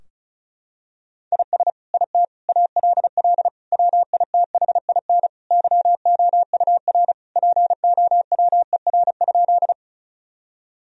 Morse code, heard as a keyed tone, about 36 wpm, 700 Hz. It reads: IS IT ALL WITHIN YOUR POWER?
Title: Sentences from Top 1000 Words - Review Part 2 - 35wpm